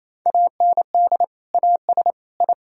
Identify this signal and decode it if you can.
Morse code, heard as a keyed tone, about 28 wpm, 700 Hz. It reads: AND AS I
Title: Sentences from Top 1000 Words - Review Part 2 - 28wpm